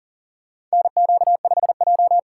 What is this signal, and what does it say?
Morse code, heard as a keyed tone, about 40 wpm, 700 Hz. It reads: NQ5J
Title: USA Call Signs - 4x Character Spacing - 40wpm